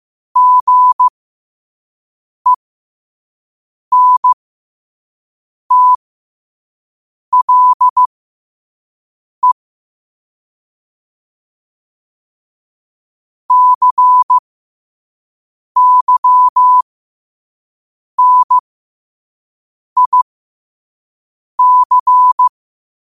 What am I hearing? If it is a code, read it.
Morse code, heard as a keyed tone, about 15 wpm, spaced as for 5 wpm, 1000 Hz. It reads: GENTLE CYNIC
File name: qotdfiles / 5wpm